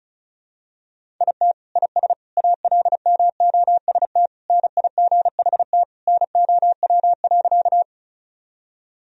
Morse code, heard as a keyed tone, about 36 wpm, 700 Hz. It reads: IT IS ALMOST NIGHT NOW.
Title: Sentences from Top 1000 Words - Review Part 4 - 35wpm